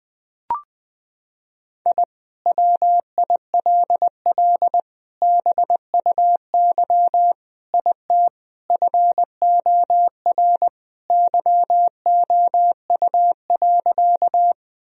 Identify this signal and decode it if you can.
Morse code, heard as a keyed tone, about 20 wpm, 700 Hz. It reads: I WILL BUY IT FOR YOU.
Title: Sentences from Top 1000 Words - Review Part 4 - 20wpm